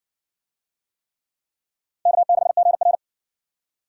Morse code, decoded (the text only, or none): K6GR